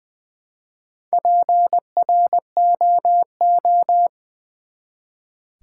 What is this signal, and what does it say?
Morse code, heard as a keyed tone, about 20 wpm, 700 Hz. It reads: PROO
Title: Build A Word - 1 Letter At A Time - 3 to 5-Letter Words - 20wpm